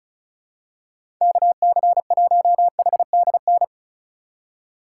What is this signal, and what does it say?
Morse code, heard as a keyed tone, about 36 wpm, 700 Hz. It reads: KC1HDN
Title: USA Call Signs - 6 Characters - 4x Character Spacing - 35wpm